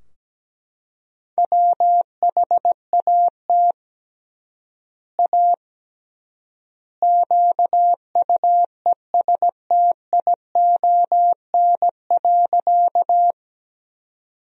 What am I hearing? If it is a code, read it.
Morse code, heard as a keyed tone, about 17 wpm, 700 Hz. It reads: WHAT A QUESTION.
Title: 3x Word-Spacing - Sentences from Top 400 Words - 17wpm